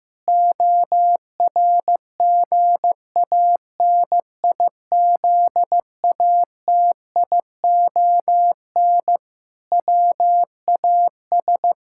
Morse code, heard as a keyed tone, about 15 wpm, 700 Hz. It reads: ORGANIZATION WAS
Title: Sentences from Top 1000 Words - Review Part 5 - 15wpm